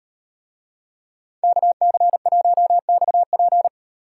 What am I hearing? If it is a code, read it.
Morse code, heard as a keyed tone, about 38 wpm, 700 Hz. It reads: KC1XP